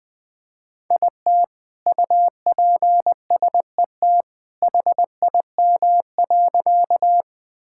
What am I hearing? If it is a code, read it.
Morse code, heard as a keyed tone, about 20 wpm, 700 Hz. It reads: IT UPSET HIM.
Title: Sentences from Top 1000 Words - Review Part 5 - 20wpm